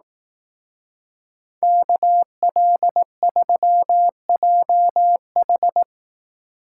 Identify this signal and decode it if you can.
Morse code, heard as a keyed tone, about 18 wpm, 700 Hz. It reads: KL3JH
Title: Speed Racing - USA Call Signs 15wpm